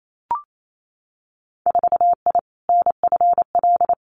Morse code, heard as a keyed tone, about 28 wpm, 700 Hz. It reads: 4I NFL
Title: Winter Field Day Exchanges 28wpm